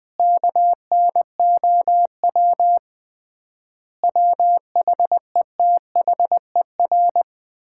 Morse code, heard as a keyed tone, about 21 wpm, 700 Hz. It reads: KNOW WHETHER
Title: 3x Word-Spacing - Sets of 5 Words - 20wpm